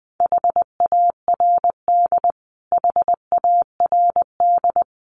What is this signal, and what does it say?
Morse code, heard as a keyed tone, about 20 wpm, 700 Hz. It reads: HARD HARD
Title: Top 300 Words Repeated 20wpm